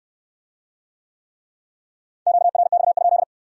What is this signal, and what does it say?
Morse code, heard as a keyed tone, about 60 wpm, 700 Hz. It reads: QRX?